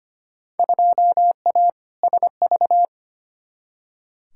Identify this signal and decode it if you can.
Morse code, heard as a keyed tone, about 25 wpm, 700 Hz. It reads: 2A SV
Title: Field Day Exchanges - 25wpm